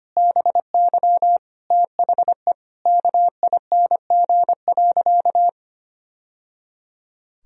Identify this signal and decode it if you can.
Morse code, heard as a keyed tone, about 25 wpm, 700 Hz. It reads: BY THE KING.
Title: Sentences from Top 1000 Words - Review Part 4 - 25wpm